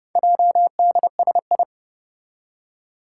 Morse code, heard as a keyed tone, about 30 wpm, 700 Hz. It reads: JDSI
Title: Four Challenging Letter-Number Combinations 30wpm